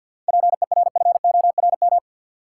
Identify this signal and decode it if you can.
Morse code, heard as a keyed tone, about 53 wpm, 700 Hz. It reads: PERFORM
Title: Sentences from Top 1500 Words - New Word Drill Alt - 50wpm